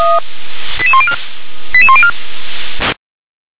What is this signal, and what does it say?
Five-tone selective call (ZVEI), heard as quick runs of five tones, tones of about 70 ms. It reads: 88104 88104